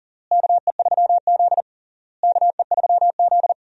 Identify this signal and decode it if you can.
Morse code, heard as a keyed tone, about 40 wpm, 700 Hz. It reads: KE3Z KE3Z
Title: USA Call Signs - 4 Characters - Repeated 3x - 40wpm